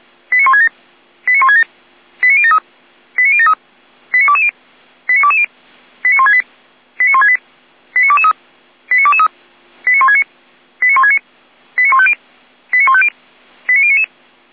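Five-tone selective call (ZVEI), heard as quick runs of five tones, tones of about 70 ms. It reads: 89167 89167 89073 89073 89220 89220 89168 89168 89223 89223 89169 89169 89150 89150 89099
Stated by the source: ZVEI five-tone call 89060 received by SDR on channel 455.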